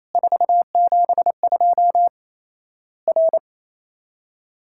Morse code, heard as a keyed tone, about 28 wpm, 700 Hz. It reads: 472 R
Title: Worked All Europe DX Contest - QTC Exchange - 28wpm